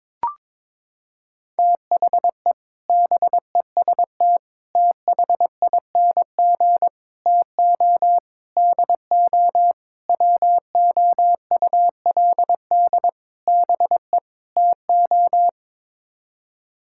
Morse code, heard as a keyed tone, about 21 wpm, 700 Hz. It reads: THE BEST THING TO DO WOULD BE TO
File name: practice / 22wpm / Sets of 8 Words - 22wpm